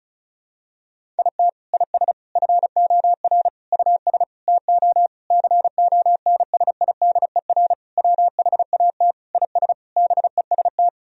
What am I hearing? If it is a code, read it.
Morse code, heard as a keyed tone, about 36 wpm, 700 Hz. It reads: IT IS FOR US TO CONSIDER WHAT IS BEST
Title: Sentences from Top 1000 Words - Review Part 4 - 35wpm